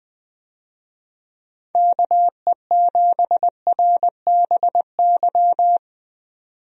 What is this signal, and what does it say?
Morse code, heard as a keyed tone, about 20 wpm, 700 Hz. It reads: KE7RBY
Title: USA Call Signs - 6 Characters - 20wpm